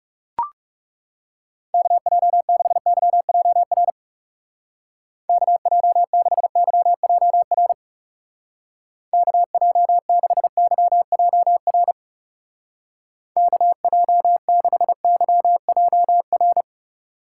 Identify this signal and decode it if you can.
Morse code, heard as a keyed tone, about 40 wpm, 700 Hz. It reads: KJ6YJR KJ6YJR KJ6YJR KJ6YJR